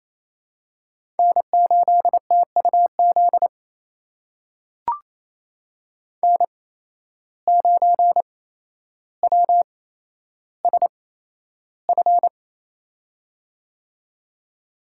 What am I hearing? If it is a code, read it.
Morse code, heard as a keyed tone, about 28 wpm, spaced as for 12 wpm, 700 Hz. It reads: N8TUZ N9WSF
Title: USA Call Signs - 5 Characters - 8x Character Spacing - 28wpm